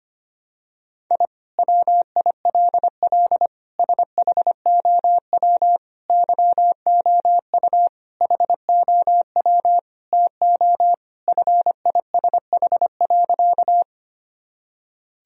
Morse code, heard as a keyed tone, about 25 wpm, 700 Hz. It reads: I WILL SHOW YOU HOW TO FISH.
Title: Sentences from Top 1000 Words - Review Part 1 - 25wpm